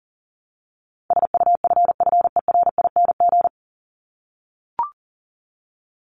Morse code, heard as a keyed tone, about 40 wpm, 700 Hz. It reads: SUFFERING
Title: ING Suffix - Top 100 Words with Base Words 40wpm